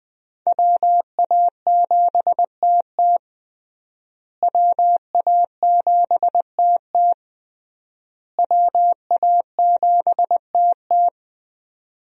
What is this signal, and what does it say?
Morse code, heard as a keyed tone, about 20 wpm, 700 Hz. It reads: WA7TT WA7TT WA7TT
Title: USA Call Signs - 5 Characters - Repeated 3x - 20wpm